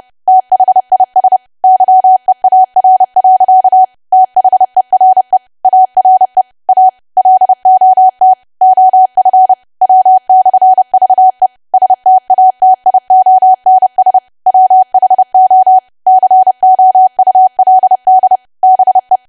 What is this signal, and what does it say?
Morse code, heard as a keyed tone, about 30 wpm, 750 Hz. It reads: THIS YEAR. THERE ARE A LOT OF W/VE STATIONS WHO COULD BE
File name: cwa / W1AW-Level3 / 30WPM